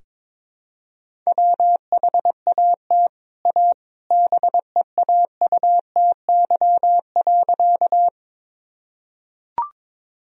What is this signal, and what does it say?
Morse code, heard as a keyed tone, about 22 wpm, 700 Hz. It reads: WHAT A BEAUTY.
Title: Sentences from Top 1000 Words - Review Part 2 - 22wpm